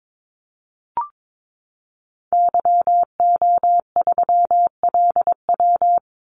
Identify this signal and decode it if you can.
Morse code, heard as a keyed tone, about 22 wpm, 700 Hz. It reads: YO3LW
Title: Call Signs Repeated 2x - 22wpm